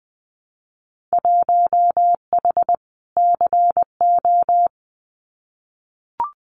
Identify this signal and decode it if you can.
Morse code, heard as a keyed tone, about 19 wpm, 700 Hz. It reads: 1H CO
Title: Winter Field Day Exchanges 20wpm